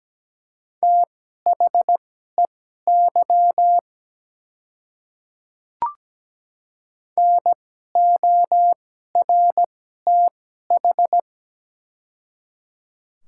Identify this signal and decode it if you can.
Morse code, heard as a keyed tone, about 17 wpm, spaced as for 12 wpm, 690 Hz. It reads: THEY NORTH
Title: 2x Character Spacing - Top 300 Words - 17wpm